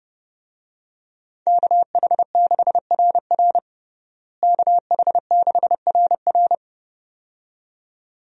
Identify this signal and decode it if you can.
Morse code, heard as a keyed tone, about 30 wpm, 700 Hz. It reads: KH6RR KH6RR
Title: USA Call Signs - 5 Characters - Repeated 2x - 30wpm